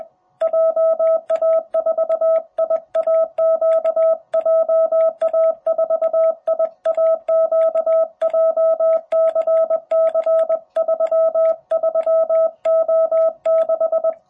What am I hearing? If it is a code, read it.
Morse code, heard as a keyed tone, about 20 wpm, 660 Hz. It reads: JA4IAQ JA4IAQ JCC33O6
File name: jh8cbh / 2021cw